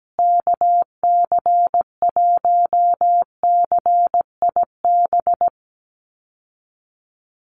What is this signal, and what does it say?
Morse code, heard as a keyed tone, about 17 wpm, 700 Hz. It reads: KC1CIB
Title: USA Call Signs Repeated 3x 17wpm